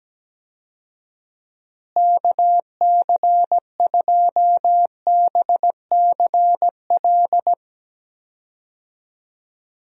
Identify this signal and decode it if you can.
Morse code, heard as a keyed tone, about 17 wpm, 700 Hz. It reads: KC2BCL
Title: USA Call Signs - 6 Characters - Repeated 2x - 17wpm